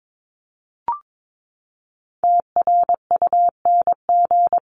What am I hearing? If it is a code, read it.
Morse code, heard as a keyed tone, about 22 wpm, 700 Hz. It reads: TRUNG
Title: CWT Exchange 22wpm